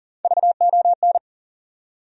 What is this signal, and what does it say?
Morse code, heard as a keyed tone, about 40 wpm, 700 Hz. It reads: VON
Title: USA Call Sign Postfixes - 3 Letters 40wpm